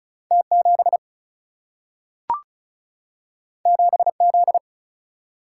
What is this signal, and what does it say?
Morse code, heard as a keyed tone, about 36 wpm, 700 Hz. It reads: T 7 7 Z
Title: Single-Letter Single-Number 35wpm